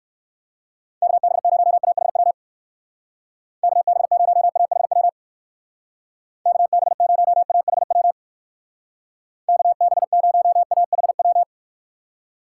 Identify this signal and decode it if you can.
Morse code, heard as a keyed tone, about 59 wpm, 700 Hz. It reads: KB0AHW KB0AHW KB0AHW KB0AHW